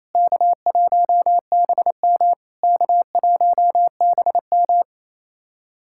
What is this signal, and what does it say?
Morse code, heard as a keyed tone, about 28 wpm, 690 Hz. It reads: K1BM K1BM
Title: SOTA QSO Practice 28wpm